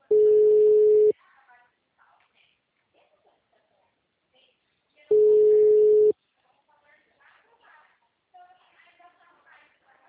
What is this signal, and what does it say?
Ringback tone (Europe, 425 Hz), with speech.